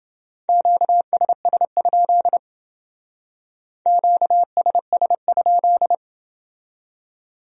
Morse code, heard as a keyed tone, about 29 wpm, 700 Hz. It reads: QSS? QSS?